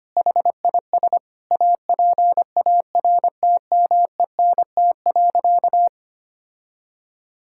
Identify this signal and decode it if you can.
Morse code, heard as a keyed tone, about 25 wpm, 700 Hz. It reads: HIS APARTMENT.
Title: Sentences from Top 1000 Words - Review Part 6 - 25wpm